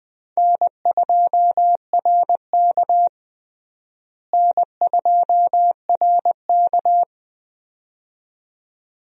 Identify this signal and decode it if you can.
Morse code, heard as a keyed tone, about 20 wpm, 700 Hz. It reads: N2RK N2RK
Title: USA Call Signs - 4 Characters - Repeated 2x - 20wpm